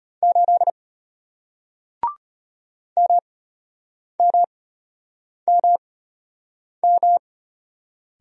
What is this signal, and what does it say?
Morse code, heard as a keyed tone, about 34 wpm, spaced as for 8 wpm, 700 Hz. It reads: 8 MMMM